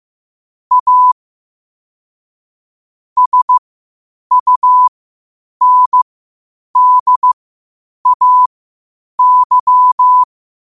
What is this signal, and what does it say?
Morse code, heard as a keyed tone, about 15 wpm, spaced as for 8 wpm, 1000 Hz. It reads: A SUNDAY
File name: qotdfiles / 8wpm